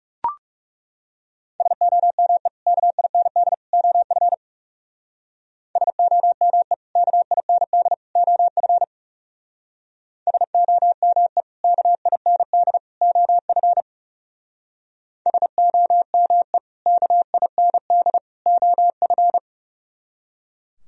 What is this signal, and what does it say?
Morse code, heard as a keyed tone, about 36 wpm, 700 Hz. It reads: SOME KIND OF SOME KIND OF SOME KIND OF SOME KIND OF
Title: Speed-Racing - Sets of 3 Words 30wpm